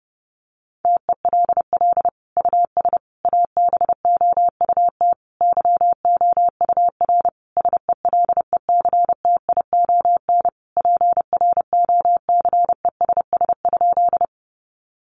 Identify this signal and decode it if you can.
Morse code, heard as a keyed tone, about 30 wpm, 700 Hz. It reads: TELL US ABOUT YOUR SELECTION PROCESS?
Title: Sentences from Top 1400 Words 30wpm